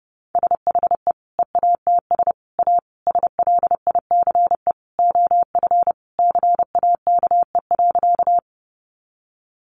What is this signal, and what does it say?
Morse code, heard as a keyed tone, about 30 wpm, 700 Hz. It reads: SHE EATS A SLICE OF CAKE.
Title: Additional Easy Phrases 30wpm